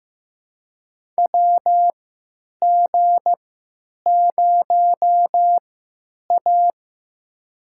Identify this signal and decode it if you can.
Morse code, heard as a keyed tone, about 15 wpm, spaced as for 9 wpm, 700 Hz. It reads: WG0A
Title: USA Call Signs - 3x Character Spacing - 15wpm